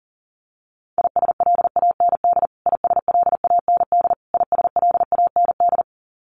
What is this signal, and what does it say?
Morse code, heard as a keyed tone, about 40 wpm, 700 Hz. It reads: ISLAND ISLAND ISLAND
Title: Top 500 Words Repeated 40wpm